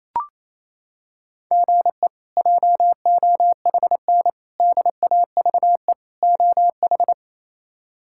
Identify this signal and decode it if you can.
Morse code, heard as a keyed tone, about 28 wpm, 700 Hz. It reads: GE JOHN DAVE OH
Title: K1USN SST QSO Practice - 28wpm